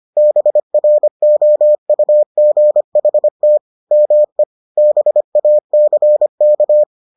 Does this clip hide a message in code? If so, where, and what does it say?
Yes: Morse code from 0.2 to 6.8 s, heard as a keyed tone, about 25 wpm, 600 Hz. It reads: BROUGHT ME BACK